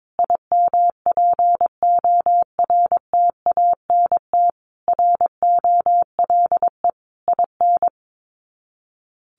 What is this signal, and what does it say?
Morse code, heard as a keyed tone, about 22 wpm, 700 Hz. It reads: IMPORTANT ROLE IN
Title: Sets of 5 Words Encore 22wpm